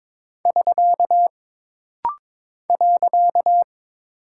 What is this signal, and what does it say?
Morse code, heard as a keyed tone, about 22 wpm, 700 Hz. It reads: <SK> .
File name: practice / 22wpm / Single Punctuation-Prosign - Rapid-Fire - 22wpm